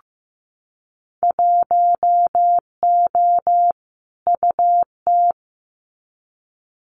Morse code, heard as a keyed tone, about 15 wpm, 700 Hz. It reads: 1O UT
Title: Winter Field Day Exchanges Encore 15wpm